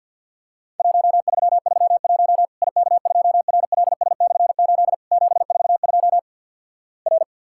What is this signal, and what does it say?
Morse code, heard as a keyed tone, about 52 wpm, 700 Hz. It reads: 1331 IK2RLS/8 742 R
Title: Worked All Europe DX Contest - QTC Exchange - 50wpm